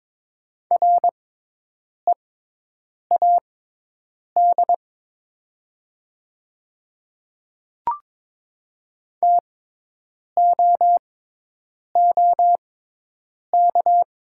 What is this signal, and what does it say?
Morse code, heard as a keyed tone, about 22 wpm, spaced as for 8 wpm, 700 Hz. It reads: READ TOOK
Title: 6x Character Spacing - Top 500 Words - 22wpm